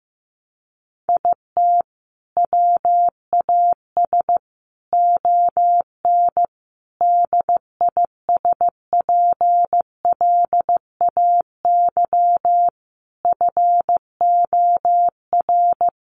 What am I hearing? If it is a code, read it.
Morse code, heard as a keyed tone, about 15 wpm, 700 Hz. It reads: IT WAS ON DISPLAY FOR
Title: Sentences from Top 1200 Words 15wpm